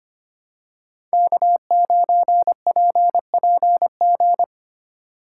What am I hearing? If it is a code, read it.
Morse code, heard as a keyed tone, about 25 wpm, 700 Hz. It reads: K9PPG
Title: USA Call Signs - 5 Characters - 6x Character Spacing - 25wpm